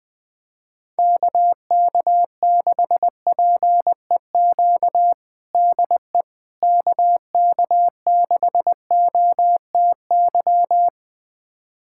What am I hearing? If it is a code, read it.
Morse code, heard as a keyed tone, about 20 wpm, 700 Hz. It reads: KK6PEQ DE KK6OTY
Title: USA Callsign DE Callsign - 6 Characters - 20wpm